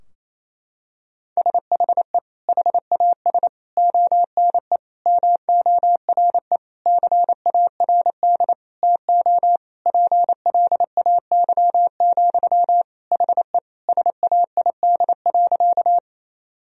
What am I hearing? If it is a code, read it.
Morse code, heard as a keyed tone, about 28 wpm, 700 Hz. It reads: SHE HAS ONE MORE CARD TO PLAY, HE SAID.
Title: Sentences from Top 1000 Words - Review Part 4 - 28wpm